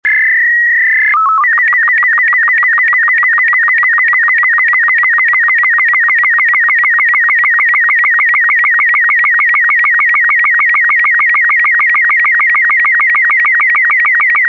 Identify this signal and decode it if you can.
SSTV (slow-scan television), Robot 36.
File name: sstv / sstv1 / robot 36